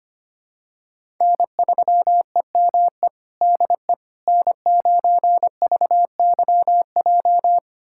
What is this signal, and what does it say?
Morse code, heard as a keyed tone, about 25 wpm, 700 Hz. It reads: N3EME DE N9VYJ
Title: USA Callsign DE Callsign - 5 Characters - 25wpm